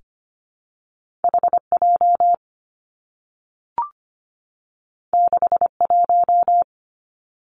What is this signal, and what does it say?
Morse code, heard as a keyed tone, about 25 wpm, 700 Hz. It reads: HJ 61
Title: Two Challenging Letter-Number Combinations 25wpm